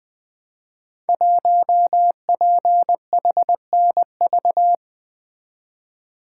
Morse code, heard as a keyed tone, about 20 wpm, 700 Hz. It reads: 1PHNV
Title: Code Groups - 5 Letter:Numbers - 20wpm